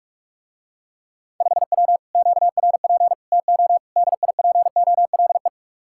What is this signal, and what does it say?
Morse code, heard as a keyed tone, about 49 wpm, 700 Hz. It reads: 5W QRP TO DIPOLE